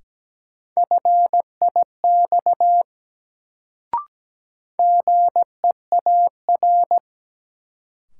Morse code, heard as a keyed tone, about 17 wpm, 700 Hz. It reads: FIX GEAR
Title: Sentences from Top 1400 Words - New Word Drill - 17wpm